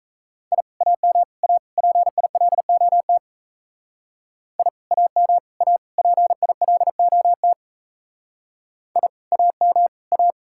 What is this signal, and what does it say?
Morse code, heard as a keyed tone, about 36 wpm, 700 Hz. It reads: I AM A PILOT I AM A PILOT I AM A